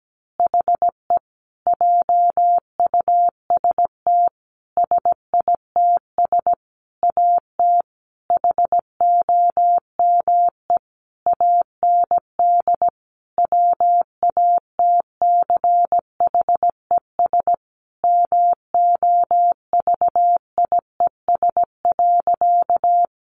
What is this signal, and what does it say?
Morse code, heard as a keyed tone, about 17 wpm, 700 Hz. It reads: HE JUST SITS AT HOME AND WATCHES MOVIES.
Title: Additional Easy Phrases 17wpm